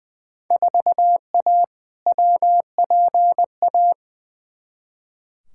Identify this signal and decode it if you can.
Morse code, heard as a keyed tone, about 20 wpm, 700 Hz. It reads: 4A WPA
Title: Field Day Exchanges Encore - 20wpm